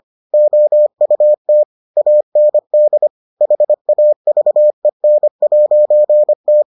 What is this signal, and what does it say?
Morse code, heard as a keyed tone, about 25 wpm, 600 Hz. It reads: OUT AND HAVEN'T